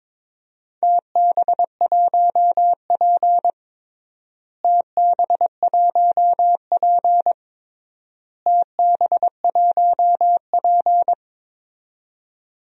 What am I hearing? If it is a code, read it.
Morse code, heard as a keyed tone, about 22 wpm, 700 Hz. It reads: TB1P TB1P TB1P
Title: DXCC-100 Call Signs - 4 Characters - Repeated 3x - 22wpm